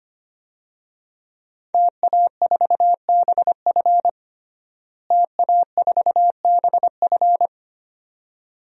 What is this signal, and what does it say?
Morse code, heard as a keyed tone, about 25 wpm, 700 Hz. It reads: TA4BF TA4BF
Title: DXCC-100 Call Signs - 5 Characters - Repeated 2x - 25wpm